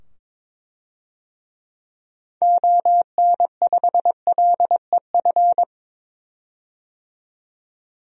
Morse code, heard as a keyed tone, about 22 wpm, 700 Hz. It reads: ON5LEF
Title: DXCC-100 Call Signs - 6 Characters - 22wpm